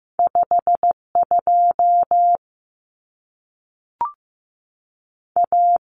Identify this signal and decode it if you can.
Morse code, heard as a keyed tone, about 15 wpm, 700 Hz. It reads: 5 2 A
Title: Common Cut Numbers - 2-Digit 15wpm